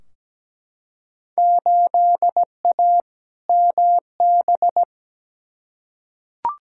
Morse code, heard as a keyed tone, about 17 wpm, 700 Hz. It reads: 8A MB
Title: Field Day Exchanges Encore - 17wpm